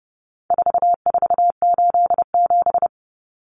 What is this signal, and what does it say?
Morse code, heard as a keyed tone, about 30 wpm, 700 Hz. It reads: 4487
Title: Numbers - 4-Digit 30wpm